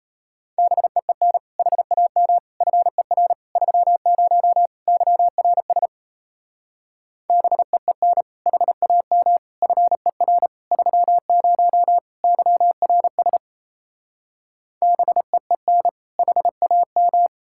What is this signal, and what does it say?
Morse code, heard as a keyed tone, about 38 wpm, 700 Hz. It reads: BEEN HAM FER 30 YRS BEEN HAM FER 30 YRS BEEN HAM